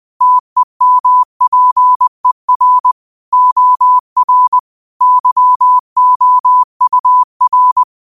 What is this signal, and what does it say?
Morse code, heard as a keyed tone, about 20 wpm, 1000 Hz. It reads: TEMPER OR YOUR
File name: qotdfiles / 20wpm